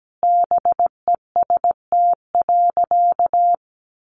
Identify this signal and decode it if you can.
Morse code, heard as a keyed tone, about 17 wpm, 700 Hz. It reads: BEST.
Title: Sentences from Top 600 Words 17wpm